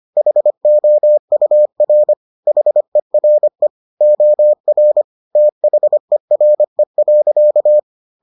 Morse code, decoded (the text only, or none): HOUR HERE OR THERE.